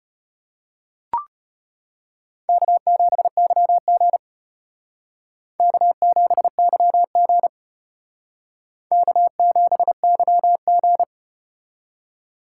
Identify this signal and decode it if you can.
Morse code, heard as a keyed tone, about 34 wpm, 700 Hz. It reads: K7YG K7YG K7YG